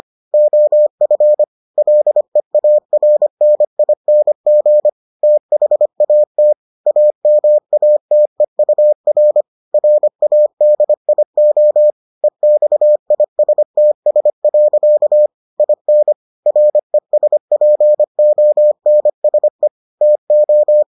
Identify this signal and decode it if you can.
Morse code, heard as a keyed tone, about 25 wpm, 600 Hz. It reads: OF LEARNING THAT AMATEUR RADIO EXISTS. IN RESPONSE TO